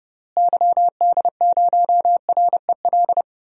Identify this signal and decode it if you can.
Morse code, heard as a keyed tone, about 30 wpm, 700 Hz. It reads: YD0REL
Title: DXCC-100 Call Signs - 6 Characters - Repeated 2x - 30wpm